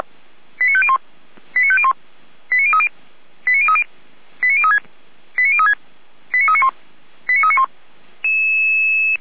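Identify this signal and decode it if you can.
Five-tone selective call (ZVEI), heard as quick runs of five tones, tones of about 70 ms. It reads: 89501 89501 89030 89030 89036 89036 89391 89391
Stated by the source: ZVEI five-tone selective call received on an SDR.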